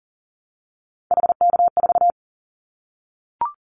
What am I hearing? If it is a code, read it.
Morse code, heard as a keyed tone, about 40 wpm, 700 Hz. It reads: HK4
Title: Callsign Prefixes for Top 100 DXCC - 40wpm